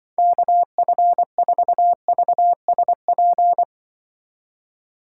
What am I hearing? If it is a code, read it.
Morse code, heard as a keyed tone, about 25 wpm, 700 Hz. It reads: KF4VSP